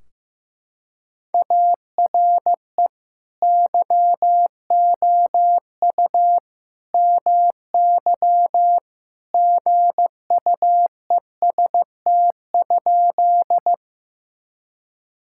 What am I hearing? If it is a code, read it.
Morse code, heard as a keyed tone, about 15 wpm, 700 Hz. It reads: ARE YOU MY GUEST?
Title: Sentences from Top 1600 Words 15wpm